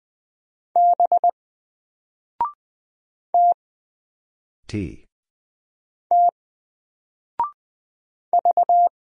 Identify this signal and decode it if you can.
Morse code, heard as a keyed tone, about 20 wpm, 700 Hz. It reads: B T T V
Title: Single Letter-Number Alt 20wpm